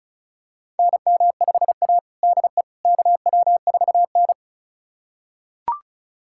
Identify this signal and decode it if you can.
Morse code, heard as a keyed tone, about 36 wpm, 700 Hz. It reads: NM5A DE KW4N
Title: USA Callsign DE Callsign - 4 Characters - 35wpm